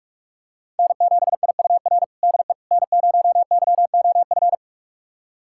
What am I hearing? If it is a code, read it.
Morse code, heard as a keyed tone, about 46 wpm, 700 Hz. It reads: N7IUR DE N0YOF
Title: USA Callsign DE Callsign - 5 Characters - 45wpm